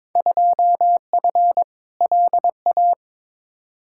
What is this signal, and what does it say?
Morse code, heard as a keyed tone, about 22 wpm, 700 Hz. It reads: 2F LA
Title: Field Day Exchanges - 22wpm